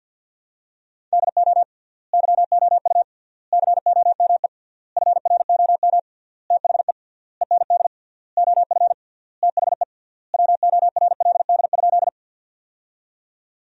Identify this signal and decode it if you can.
Morse code, heard as a keyed tone, about 50 wpm, 700 Hz. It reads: DO YOU COME FROM THE END OF THE WORLD?
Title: 3x Word-Spacing - Sentences from Top 200 Words - 50wpm